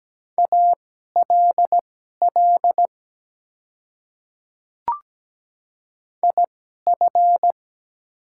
Morse code, heard as a keyed tone, about 17 wpm, spaced as for 12 wpm, 700 Hz. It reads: ALL IF
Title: 2x Character Spacing - Top 100 Words - 17wpm